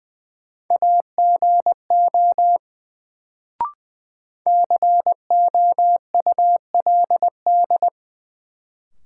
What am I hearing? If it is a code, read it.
Morse code, heard as a keyed tone, about 20 wpm, 700 Hz. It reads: AGO COULD
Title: Sets of 2 Words - New Word Drill - 20wpm